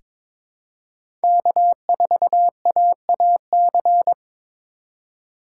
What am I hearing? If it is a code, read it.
Morse code, heard as a keyed tone, about 21 wpm, 700 Hz. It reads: K4AAC
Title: K1USN SST QSO Practice - 22wpm